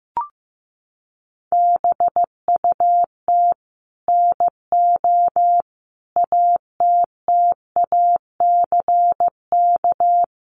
Morse code, heard as a keyed tone, about 15 wpm, 700 Hz. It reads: BUT NO ATTACK
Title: Sentences from Top 1500 Words 15wpm